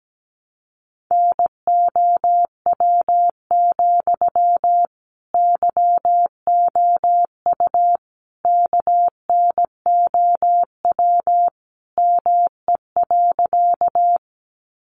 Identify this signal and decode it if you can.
Morse code, heard as a keyed tone, about 17 wpm, 700 Hz. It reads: NOW, YOU KNOW ME.
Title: Sentences from Top 200 Words 17wpm